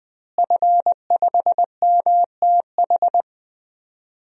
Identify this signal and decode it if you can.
Morse code, heard as a keyed tone, about 20 wpm, 700 Hz. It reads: F5MTH
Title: DXCC-100 Call Signs - 5 Characters - Repeated 2x - 20wpm